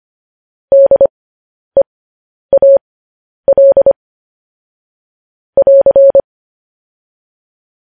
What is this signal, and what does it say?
Morse code, heard as a keyed tone, about 25 wpm, spaced as for 10 wpm, 550 Hz. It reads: DEAL <AR>
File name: podcast / News headlines-1x-25x10